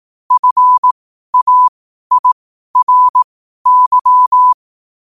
Morse code, heard as a keyed tone, about 18 wpm, spaced as for 12 wpm, 1000 Hz. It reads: FAIRY
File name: qotdfiles / 12wpm